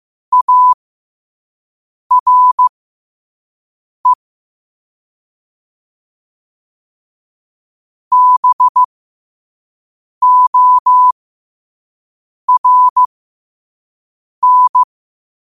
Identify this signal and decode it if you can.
Morse code, heard as a keyed tone, about 15 wpm, spaced as for 5 wpm, 1000 Hz. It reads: ARE BORN
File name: qotdfiles / 5wpm